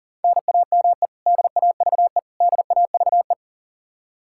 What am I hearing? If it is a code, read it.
Morse code, heard as a keyed tone, about 40 wpm, 700 Hz. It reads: NAME DAVE DAVE
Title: Practice QSO - Exchange 1 - Beginning - 40wpm